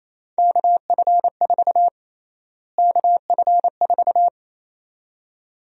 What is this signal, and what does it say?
Morse code, heard as a keyed tone, about 28 wpm, 700 Hz. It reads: KF4 KF4
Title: USA Call Sign Prefixes - Repeated 2x - 28wpm